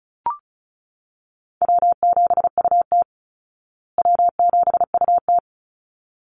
Morse code, heard as a keyed tone, about 34 wpm, 700 Hz. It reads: W7UT W7UT
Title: Call Signs Repeated 3x - 35wpm